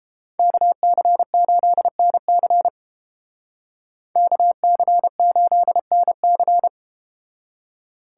Morse code, heard as a keyed tone, about 32 wpm, 700 Hz. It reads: KC8NC KC8NC